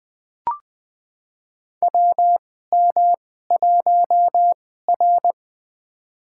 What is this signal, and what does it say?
Morse code, heard as a keyed tone, about 21 wpm, spaced as for 15 wpm, 700 Hz. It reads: WM1R
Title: USA Call Signs - 4 Characters - 2x Character Spacing - 20wpm